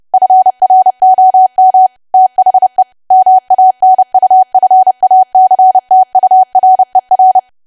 Morse code, heard as a keyed tone, about 30 wpm, 750 Hz. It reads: FROM THE MANUFACTURER
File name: cwa / W1AW-Level3 / 30WPM